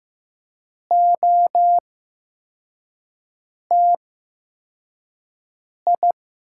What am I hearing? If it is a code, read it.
Morse code, heard as a keyed tone, about 15 wpm, spaced as for 4 wpm, 700 Hz. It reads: OTI
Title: 8x Character Spacing - Top 500 Words - 15wpm